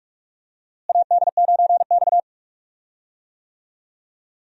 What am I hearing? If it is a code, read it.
Morse code, heard as a keyed tone, about 44 wpm, 700 Hz. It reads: AD9X
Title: USA Call Signs - 4 Characters - Repeated 2x - 45wpm